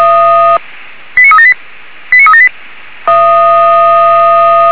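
Five-tone selective call (ZVEI), heard as quick runs of five tones, tones of about 70 ms. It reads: 89378 89378
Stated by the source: ZVEI five-tone selective call received by SDR.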